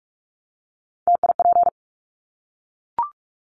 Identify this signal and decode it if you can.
Morse code, heard as a keyed tone, about 44 wpm, 700 Hz. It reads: TIP
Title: Top 1501-1750 Words 45wpm